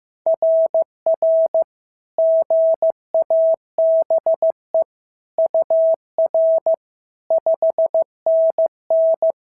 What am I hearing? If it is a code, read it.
Morse code, heard as a keyed tone, about 14 wpm, 650 Hz. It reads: RR GABE UR 5NN
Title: SOTA QSO Practice 15wpm